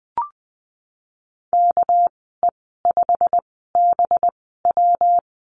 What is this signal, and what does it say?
Morse code, heard as a keyed tone, about 20 wpm, spaced as for 15 wpm, 700 Hz. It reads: KE5BW
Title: 200 Lesson 32g - Letter k - Call Signs 20wpm